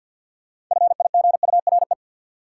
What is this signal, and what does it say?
Morse code, heard as a keyed tone, about 46 wpm, 700 Hz. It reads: FIGURE
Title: Sentences from Top 1400 Words - New Word Drill Alt - 50wpm